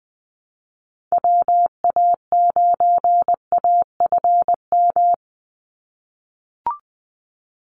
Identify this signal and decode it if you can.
Morse code, heard as a keyed tone, about 20 wpm, 700 Hz. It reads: WA9AFM
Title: USA Call Signs 20wpm